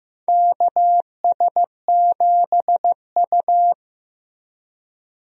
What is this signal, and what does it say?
Morse code, heard as a keyed tone, about 15 wpm, 700 Hz. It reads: KS7U
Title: USA Call Signs - 4 Characters - Repeated 3x - 15wpm